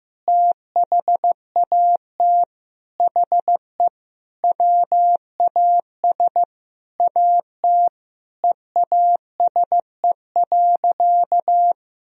Morse code, heard as a keyed tone, about 15 wpm, 700 Hz. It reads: THAT HE WAS AT EASE.
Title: Sentences from Top 1000 Words - Review Part 1 - 15wpm